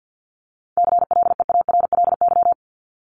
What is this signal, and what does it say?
Morse code, heard as a keyed tone, about 50 wpm, 700 Hz. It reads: CLEARLY
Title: Top 1001-1250 Words 50wpm